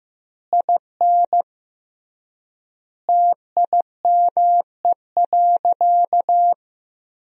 Morse code, heard as a keyed tone, about 16 wpm, 700 Hz. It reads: IN TIME.
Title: 3x Word-Spacing - Sentences from Top 300 Words - 15wpm